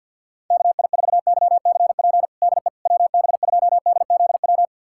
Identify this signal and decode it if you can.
Morse code, heard as a keyed tone, about 52 wpm, 700 Hz. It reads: KI4YCP DE WB2DZW
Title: USA Callsign DE Callsign - 6 Characters - 50wpm